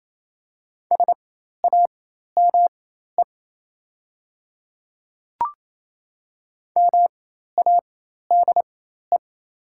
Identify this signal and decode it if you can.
Morse code, heard as a keyed tone, about 28 wpm, spaced as for 13 wpm, 700 Hz. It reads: SAME MADE
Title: 4x Character Spacing - Top 200 Words - 28wpm